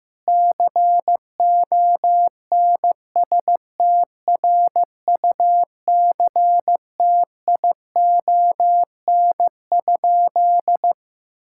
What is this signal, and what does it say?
Morse code, heard as a keyed tone, about 15 wpm, 700 Hz. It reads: CONSTRUCTION?
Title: Sentences from Top 1600 Words 15wpm